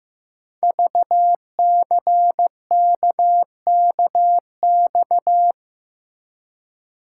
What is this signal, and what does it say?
Morse code, heard as a keyed tone, about 15 wpm, 700 Hz. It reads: VCKKX
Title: Code Groups - 5 Letters - 15wpm